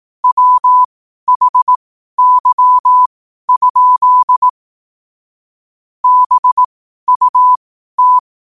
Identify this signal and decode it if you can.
Morse code, heard as a keyed tone, about 18 wpm, spaced as for 12 wpm, 1000 Hz. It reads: WHY? BUT
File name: qotdfiles / 12wpm